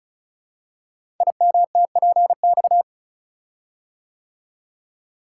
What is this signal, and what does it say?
Morse code, heard as a keyed tone, about 36 wpm, 700 Hz. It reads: IMTPX
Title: Code Groups - 5 Letters - 35wpm